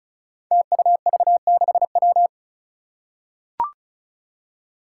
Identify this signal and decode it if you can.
Morse code, heard as a keyed tone, about 36 wpm, 700 Hz. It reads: TUV6W
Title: Five Challenging Letter-Number Combinations 35wpm